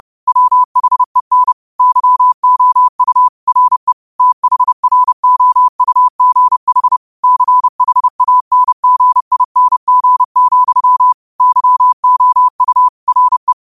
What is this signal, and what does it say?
Morse code, heard as a keyed tone, about 30 wpm, 1000 Hz. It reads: WHEN YOU RE THROUGH CHANGING, YOU RE